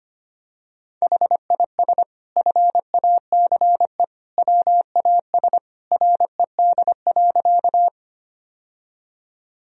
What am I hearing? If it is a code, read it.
Morse code, heard as a keyed tone, about 25 wpm, 700 Hz. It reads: HIS FACE WAS RED.
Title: Sentences from Top 1000 Words - Review Part 2 - 25wpm